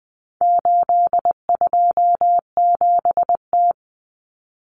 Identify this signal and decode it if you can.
Morse code, heard as a keyed tone, about 20 wpm, 700 Hz. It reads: 827T
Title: Common Cut Numbers - 4-Digit 20wpm